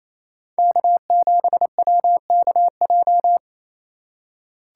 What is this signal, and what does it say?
Morse code, heard as a keyed tone, about 27 wpm, 700 Hz. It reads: K7WKJ
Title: USA Call Signs - 5 Characters - 8x Character Spacing - 28wpm